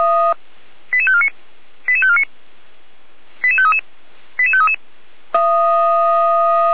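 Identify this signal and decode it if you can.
Five-tone selective call (ZVEI), heard as quick runs of five tones, tones of about 70 ms. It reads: 88549 88549 88533 88533
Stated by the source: ZVEI five-tone selective call 88417 received by SDR, recorded by Feuerwehr Orlamünde.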